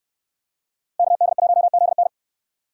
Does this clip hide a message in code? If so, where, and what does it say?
Yes: Morse code from 1.0 to 2.1 s, heard as a keyed tone, about 64 wpm, 700 Hz. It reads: KD1ZN